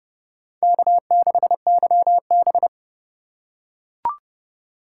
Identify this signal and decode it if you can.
Morse code, heard as a keyed tone, about 29 wpm, 690 Hz. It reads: K6YB